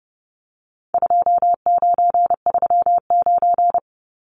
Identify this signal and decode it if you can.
Morse code, heard as a keyed tone, about 30 wpm, 700 Hz. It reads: 2939
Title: Numbers - 4-Digit 30wpm